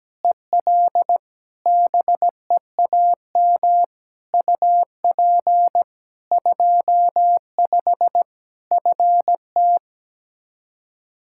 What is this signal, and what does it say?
Morse code, heard as a keyed tone, about 17 wpm, 700 Hz. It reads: EL BEAM UP 25 FT
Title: Practice QSO - Exchange 2 - Info - 17wpm